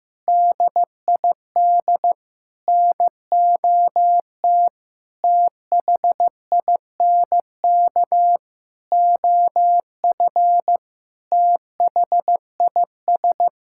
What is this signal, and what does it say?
Morse code, heard as a keyed tone, about 15 wpm, 700 Hz. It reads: DID NOT THINK OF THIS
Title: Sentences from Top 1600 Words 15wpm